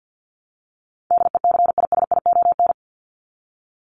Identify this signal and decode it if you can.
Morse code, heard as a keyed tone, about 50 wpm, 700 Hz. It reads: DECISION
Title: Top 751-1000 Words 50wpm